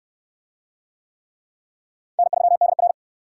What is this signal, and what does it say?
Morse code, heard as a keyed tone, about 68 wpm, 700 Hz. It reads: N3DR